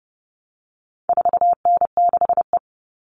Morse code, heard as a keyed tone, about 30 wpm, 700 Hz. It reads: 4N6E
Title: All Cut Numbers - 4-Digit 30wpm